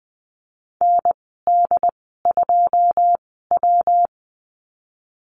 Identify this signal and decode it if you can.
Morse code, heard as a keyed tone, about 20 wpm, spaced as for 14 wpm, 700 Hz. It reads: ND2W
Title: 161 Lesson 26g - Letter g - Call Signs 20wpm